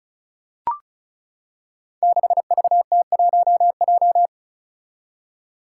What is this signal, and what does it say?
Morse code, heard as a keyed tone, about 36 wpm, 700 Hz. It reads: 6VT1J
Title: Five Challenging Letter-Number Combinations 35wpm